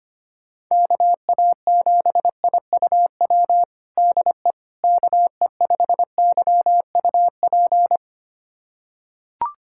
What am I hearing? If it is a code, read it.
Morse code, heard as a keyed tone, about 25 wpm, 700 Hz. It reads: KA7IUW DE KE5YUP
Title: USA Callsign DE Callsign - 6 Characters - 25wpm